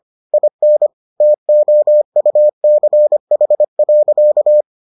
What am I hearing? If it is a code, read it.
Morse code, heard as a keyed tone, about 25 wpm, 600 Hz. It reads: IN TOUCH.